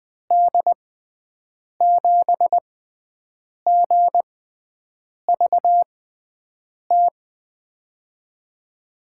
Morse code, heard as a keyed tone, about 20 wpm, spaced as for 7 wpm, 700 Hz. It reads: D7GVT
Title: USA Call Signs - 6 Characters - 6x Character Spacing - 20wpm